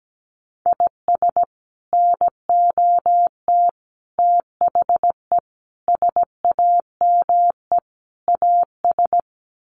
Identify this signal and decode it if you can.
Morse code, heard as a keyed tone, about 17 wpm, 700 Hz. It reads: IS NOT THE SAME AS
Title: Sets of 5 Words Encore 17wpm